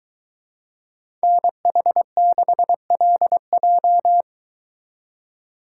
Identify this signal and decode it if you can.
Morse code, heard as a keyed tone, about 23 wpm, 700 Hz. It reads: NH6LJ